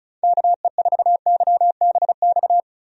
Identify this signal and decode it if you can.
Morse code, heard as a keyed tone, about 34 wpm, 700 Hz. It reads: KE4YBX
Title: USA Call Signs - 6 Characters - 35wpm